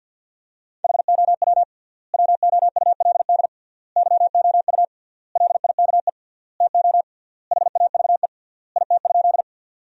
Morse code, heard as a keyed tone, about 53 wpm, 700 Hz. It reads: HOW WOULD YOU LIKE TO HAVE IT?
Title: 3x Word-Spacing - Sentences from Top 100 Words - 50wpm